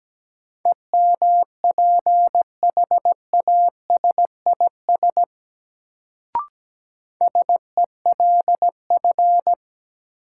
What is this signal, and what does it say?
Morse code, heard as a keyed tone, about 17 wpm, 700 Hz. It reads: EMPHASIS SELF
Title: Sentences from Top 1400 Words - New Word Drill Alt - 17wpm